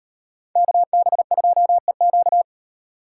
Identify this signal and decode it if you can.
Morse code, heard as a keyed tone, about 38 wpm, 700 Hz. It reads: KB2EQ